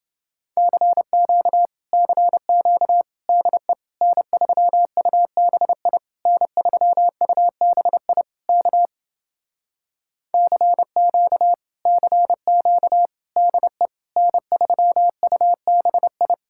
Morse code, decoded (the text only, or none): CQ CQ DE N3UBI N3UBI K CQ CQ DE N3UBI